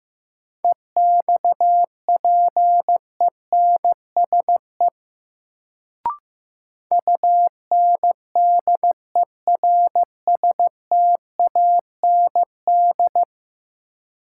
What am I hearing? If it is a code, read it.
Morse code, heard as a keyed tone, about 15 wpm, 700 Hz. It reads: EXPENSE UNDERSTAND
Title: Sets of 4 Words - New Word Drill - 15wpm